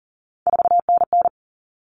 Morse code, heard as a keyed tone, about 40 wpm, 700 Hz. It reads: 4NN
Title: All Cut Numbers - 3-Digit 40wpm